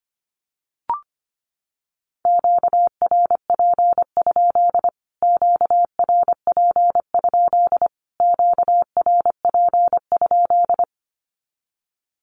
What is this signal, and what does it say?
Morse code, heard as a keyed tone, about 25 wpm, 700 Hz. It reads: QRP? QRP? QRP?
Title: Q-Codes Repeated 3x 25wpm